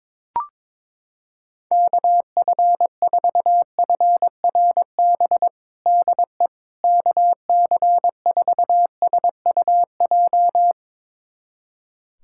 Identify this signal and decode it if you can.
Morse code, heard as a keyed tone, about 22 wpm, 700 Hz. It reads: KF4FRB DE KC4SUJ
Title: USA Callsign DE Callsign - 6 Characters - 22wpm